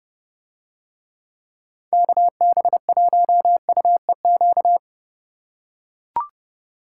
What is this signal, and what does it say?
Morse code, heard as a keyed tone, about 29 wpm, 700 Hz. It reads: KB1UEQ